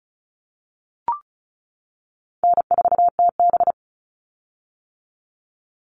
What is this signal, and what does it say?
Morse code, heard as a keyed tone, about 34 wpm, 700 Hz. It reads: N4TB
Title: Call Signs - 35wpm